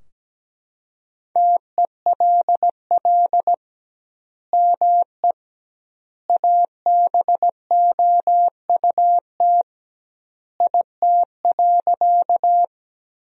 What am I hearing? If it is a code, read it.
Morse code, heard as a keyed tone, about 17 wpm, 700 Hz. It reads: TELL ME ABOUT IT.
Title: 2x Word-Spacing - Sentences from Top 200 Words - 17wpm